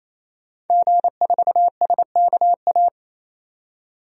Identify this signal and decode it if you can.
Morse code, heard as a keyed tone, about 27 wpm, 700 Hz. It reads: G4SKA
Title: DXCC-100 Call Signs - 5 Characters - 28wpm